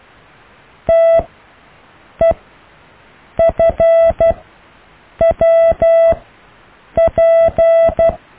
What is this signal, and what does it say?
Morse code, heard as a keyed tone, about 12 wpm, spaced as for 7 wpm, 660 Hz. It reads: TEFWP
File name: cw / EKTPV+BFHNW